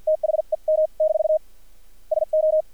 Morse code, heard as a keyed tone, about 50 wpm, spaced as for 41 wpm, 620 Hz. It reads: THEM- SO
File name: archived / CRSnet